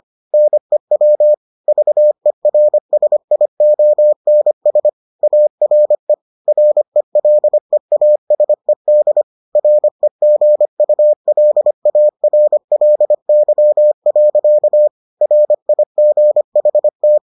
Morse code, heard as a keyed tone, about 25 wpm, 600 Hz. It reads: NEW VERSIONS ARE RELEASED REGULARLY. RIGHT